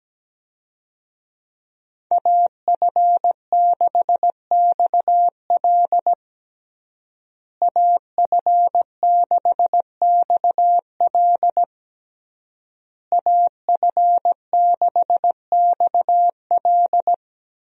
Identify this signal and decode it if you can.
Morse code, heard as a keyed tone, about 17 wpm, 700 Hz. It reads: AF6XL AF6XL AF6XL
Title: USA Call Signs - 5 Characters - Repeated 3x - 17wpm